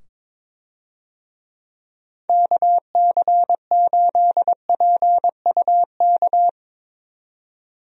Morse code, heard as a keyed tone, about 22 wpm, 700 Hz. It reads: KC8PUK
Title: USA Call Signs - 6 Characters - 22wpm